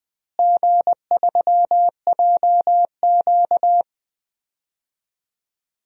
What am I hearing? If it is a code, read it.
Morse code, heard as a keyed tone, about 20 wpm, 700 Hz. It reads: G3JQ
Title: DXCC-100 Call Signs - 4 Characters - Repeated 3x - 20wpm